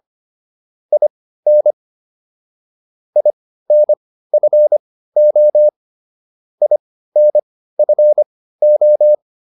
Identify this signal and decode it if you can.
Morse code, heard as a keyed tone, about 25 wpm, spaced as for 15 wpm, 600 Hz. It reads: IN INFO INFO